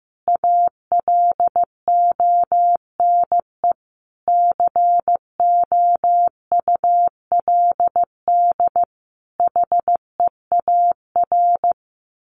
Morse code, heard as a keyed tone, about 15 wpm, 700 Hz. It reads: ALONE COULD HEAR
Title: Sentences from Top 1000 Words 15wpm